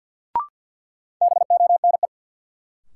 Morse code, heard as a keyed tone, about 53 wpm, 700 Hz. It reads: BONE
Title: Sentences from Top 1600 Words - New Word Drill - 50wpm